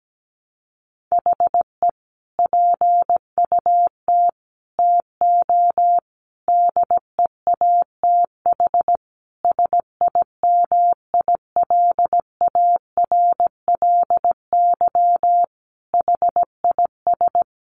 Morse code, heard as a keyed tone, about 17 wpm, 700 Hz. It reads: HE PUT TO DEATH SIMILARLY HIS
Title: Sentences from Top 1500 Words 17wpm